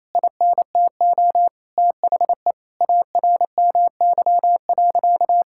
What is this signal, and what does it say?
Morse code, heard as a keyed tone, about 28 wpm, 700 Hz. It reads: INTO THE ARMY.
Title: Sentences from Top 1000 Words - Review Part 5 - 28wpm